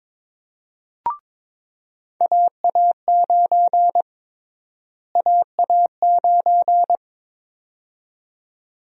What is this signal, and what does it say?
Morse code, heard as a keyed tone, about 22 wpm, 700 Hz. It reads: AA9 AA9
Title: USA Call Sign Prefixes - Repeated 2x - 22wpm